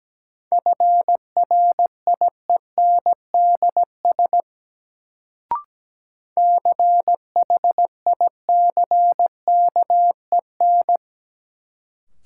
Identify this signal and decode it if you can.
Morse code, heard as a keyed tone, about 17 wpm, 700 Hz. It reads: FRIENDS CHICKEN
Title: Sentences from Top 1000 Words - Review Part 5 - New Word Drill - 17wpm